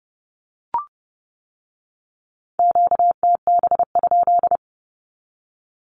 Morse code, heard as a keyed tone, about 30 wpm, 700 Hz. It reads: QTB?
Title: Q-Codes with Spoken Meaning - 30wpm